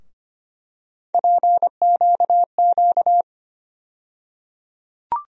Morse code, heard as a keyed tone, about 25 wpm, 700 Hz. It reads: PQQ
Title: USA Call Sign Postfixes - 3 Letters 25wpm